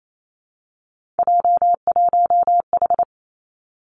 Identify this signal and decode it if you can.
Morse code, heard as a keyed tone, about 28 wpm, 700 Hz. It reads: J1H
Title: Three Challenging Letter-Number Combinations 28wpm